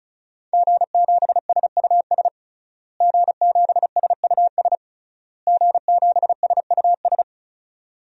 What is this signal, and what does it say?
Morse code, heard as a keyed tone, about 36 wpm, 700 Hz. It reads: G7SUS G7SUS G7SUS
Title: DXCC-100 Call Signs - 5 Characters - Repeated 3x - 35wpm